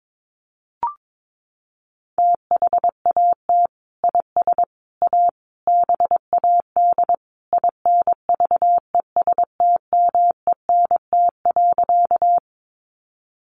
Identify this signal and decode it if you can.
Morse code, heard as a keyed tone, about 22 wpm, 700 Hz. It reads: THAT IS A BAD INVESTMENT.
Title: Sentences from Top 700 Words 22wpm